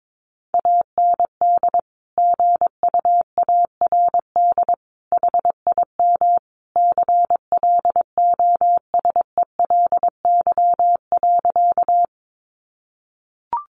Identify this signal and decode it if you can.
Morse code, heard as a keyed tone, about 22 wpm, 700 Hz. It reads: AND GUARD HIM CLOSELY.
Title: Sentences from Top 1400 Words 22wpm